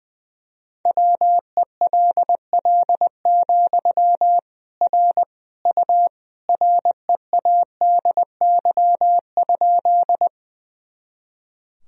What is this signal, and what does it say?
Morse code, heard as a keyed tone, about 20 wpm, 700 Hz. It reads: WELL, R U READY?
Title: Sentences from Top 400 Words and Abbreviations 20wpm